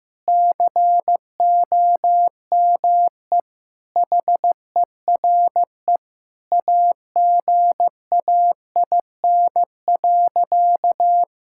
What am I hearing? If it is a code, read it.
Morse code, heard as a keyed tone, about 15 wpm, 700 Hz. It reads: COME HERE AGAIN.
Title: Sentences from Top 1600 Words 15wpm